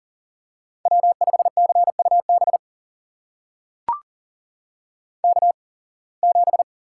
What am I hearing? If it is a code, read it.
Morse code, heard as a keyed tone, about 39 wpm, spaced as for 26 wpm, 700 Hz. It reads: W5CUB K7
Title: USA Call Signs - 5 Characters - 8x Character Spacing - 40wpm